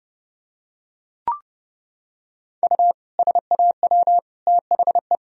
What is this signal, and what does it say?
Morse code, heard as a keyed tone, about 30 wpm, 700 Hz. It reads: U SAW THE
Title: Sentences from Top 400 Words and Abbreviations 30wpm